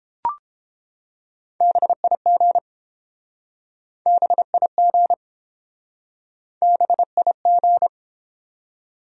Morse code, heard as a keyed tone, about 29 wpm, 700 Hz. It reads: BIG BIG BIG